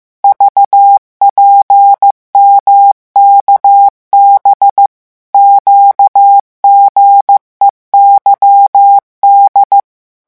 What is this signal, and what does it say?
Morse code, heard as a keyed tone, about 15 wpm, 800 Hz. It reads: VPMKB QGEYD
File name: morse mp3 / revision7-vitesse 14 mots